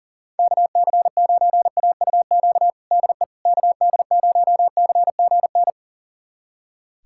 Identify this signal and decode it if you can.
Morse code, heard as a keyed tone, about 40 wpm, 700 Hz. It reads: KC9AUQ DE KD0CGN
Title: USA Callsign DE Callsign - 6 Characters - 40wpm